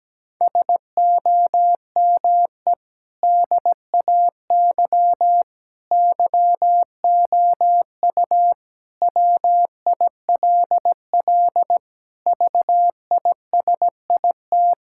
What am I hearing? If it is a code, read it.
Morse code, heard as a keyed tone, about 17 wpm, 700 Hz. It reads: SOME DAY YOU WILL VISIT
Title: Sentences from Top 1000 Words - Review Part 6 - 17wpm